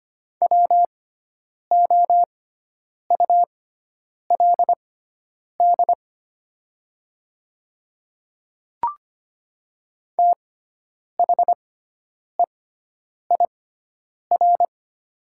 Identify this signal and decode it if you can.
Morse code, heard as a keyed tone, about 25 wpm, spaced as for 9 wpm, 690 Hz. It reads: WOULD THEIR
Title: 6x Character Spacing - Top 300 Words - 25wpm